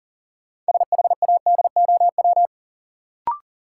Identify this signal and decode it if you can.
Morse code, heard as a keyed tone, about 40 wpm, 700 Hz. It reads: SHADOW
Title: Sets of 5 Words Encore - New Word Drill - 40wpm